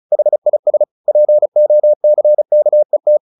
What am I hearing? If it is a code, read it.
Morse code, heard as a keyed tone, about 34 wpm, 600 Hz. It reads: HIS POCKET
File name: the old man and the sea / the old man and the sea 1 35wpm